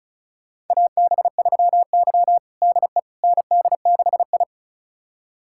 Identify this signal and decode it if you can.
Morse code, heard as a keyed tone, about 34 wpm, 700 Hz. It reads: AB3Y DE ND6I
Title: USA Callsign DE Callsign - 4 Characters - 35wpm